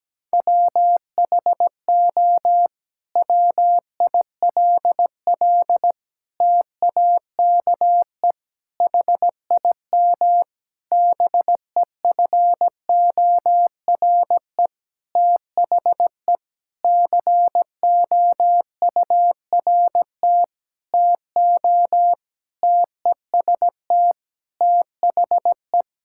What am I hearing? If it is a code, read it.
Morse code, heard as a keyed tone, about 17 wpm, 700 Hz. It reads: WHO WILL TAKE HIM BEFORE THE COURT TO TEST THE
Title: Sentences from Top 1600 Words 17wpm